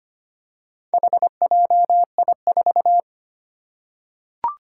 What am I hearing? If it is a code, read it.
Morse code, heard as a keyed tone, about 25 wpm, 700 Hz. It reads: HJI4
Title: Four Challenging Letter-Number Combinations 25wpm